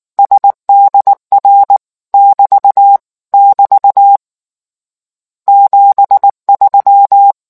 Morse code, heard as a keyed tone, about 19 wpm, 800 Hz. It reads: SDR <BT> <BT> 73